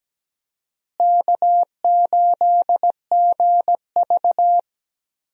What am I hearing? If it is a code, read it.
Morse code, heard as a keyed tone, about 17 wpm, 700 Hz. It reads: K8GV
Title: USA Call Signs - 4 Characters - Repeated 2x - 17wpm